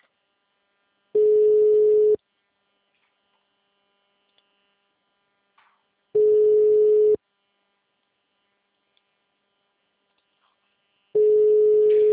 Ringback tone (Europe, 425 Hz).
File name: out- -2013- - - .39755